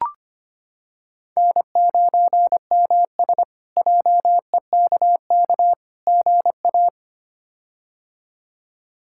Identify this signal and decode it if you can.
Morse code, heard as a keyed tone, about 25 wpm, 700 Hz. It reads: N9MS JEKK GA
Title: K1USN SST QSO Practice - 25wpm